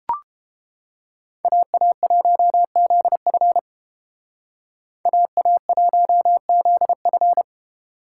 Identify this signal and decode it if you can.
Morse code, heard as a keyed tone, about 29 wpm, 700 Hz. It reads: AA1ZF AA1ZF